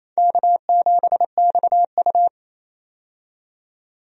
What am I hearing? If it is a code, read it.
Morse code, heard as a keyed tone, about 28 wpm, 700 Hz. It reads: K7XU
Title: USA Call Signs - 4 Characters - 28wpm